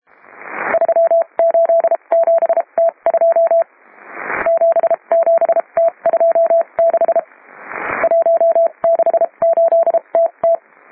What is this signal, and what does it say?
Morse code, heard as a keyed tone, about 32 wpm, spaced as for 26 wpm, 670 Hz, with radio noise. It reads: 387T2 77T26 168TT